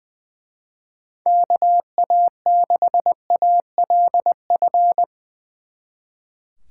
Morse code, heard as a keyed tone, about 22 wpm, 700 Hz. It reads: KA6ALF
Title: Speed Racing - USA 2x3 Call Signs 20wpm